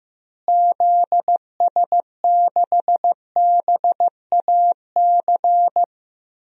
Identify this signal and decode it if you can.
Morse code, heard as a keyed tone, about 15 wpm, 700 Hz. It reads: ZS6BAC
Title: DXCC-100 Call Signs - 6 Characters - Repeated 3x - 15wpm